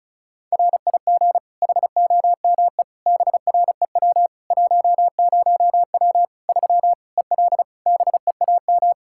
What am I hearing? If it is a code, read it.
Morse code, heard as a keyed tone, about 36 wpm, 700 Hz. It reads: RIG HOME BREW 10W 3 EL BEAM
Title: K1USN SST QSO Practice 35wpm